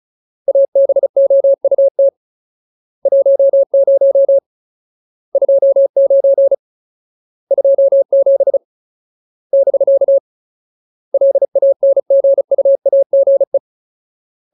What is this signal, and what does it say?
Morse code, heard as a keyed tone, about 34 wpm, 550 Hz. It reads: ABOUT 10 29 27 <BK> LANGUAGE
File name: Ultimate Word List / 35WPM Ultimate